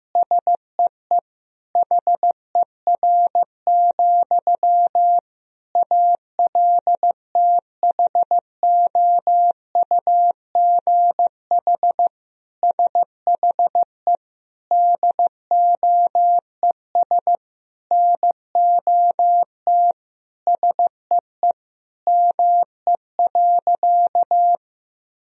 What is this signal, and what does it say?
Morse code, heard as a keyed tone, about 15 wpm, 700 Hz. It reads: SEE HER, ALTHOUGH SHE DOES NOT SEE ME.
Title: Sentences from Top 1000 Words - Review Part 3 - 15wpm